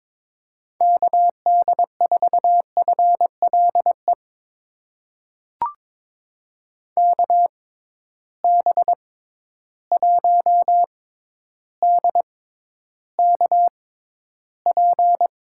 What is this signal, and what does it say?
Morse code, heard as a keyed tone, about 22 wpm, spaced as for 11 wpm, 700 Hz. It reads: KD4FLE KB1DKP
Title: USA Call Signs - 6x Character Spacing - 22wpm